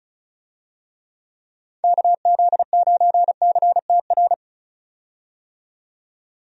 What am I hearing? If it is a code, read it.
Morse code, heard as a keyed tone, about 34 wpm, 700 Hz. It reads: KZ9CTR
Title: USA Call Signs - 6 Characters - 35wpm